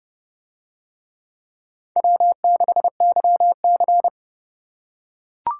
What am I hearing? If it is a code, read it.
Morse code, heard as a keyed tone, about 28 wpm, 700 Hz. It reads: W6YC